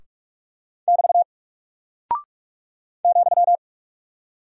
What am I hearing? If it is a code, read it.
Morse code, heard as a keyed tone, about 44 wpm, 700 Hz. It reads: <BT> ,
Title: Single Character (Letter:Number:Punctuation:Prosign) 45wpm